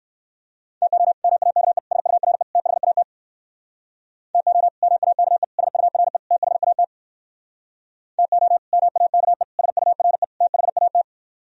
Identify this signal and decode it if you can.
Morse code, heard as a keyed tone, about 68 wpm, 700 Hz. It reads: TO MAKE SURE THAT TO MAKE SURE THAT TO MAKE SURE THAT